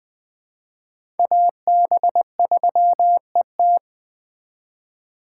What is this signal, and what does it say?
Morse code, heard as a keyed tone, about 20 wpm, 700 Hz. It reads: AB3ET
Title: USA Call Signs - 5 Characters - 4x Character Spacing - 20wpm